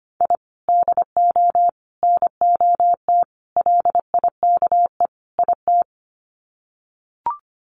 Morse code, heard as a keyed tone, about 25 wpm, 700 Hz. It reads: I DO NOT LIKE IT
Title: Sets of 5 Words Encore 25wpm